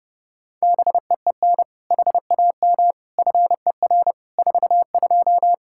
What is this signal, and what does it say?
Morse code, heard as a keyed tone, about 32 wpm, 700 Hz. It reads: BEEN HAM FER 42
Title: Practice QSO - Exchange 3 - More Info - 30wpm